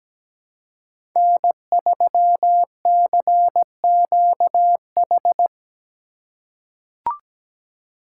Morse code, heard as a keyed tone, about 17 wpm, 700 Hz. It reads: N3CQH
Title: USA Call Signs - 6x Character Spacing - 17wpm